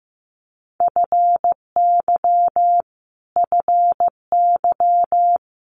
Morse code, heard as a keyed tone, about 15 wpm, 700 Hz. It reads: FY FY
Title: Multi-letter Phonemes Repeated 3x 15wpm